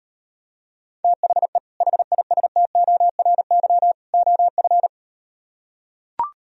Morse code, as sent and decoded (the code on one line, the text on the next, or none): - .... / .... .. ... - --- .-. -.-- / --- ..-.
THE HISTORY OF